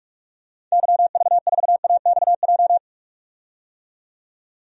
Morse code, heard as a keyed tone, about 47 wpm, 700 Hz. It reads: YV4AXJ
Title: DXCC-100 Call Signs - 6 Characters - 45wpm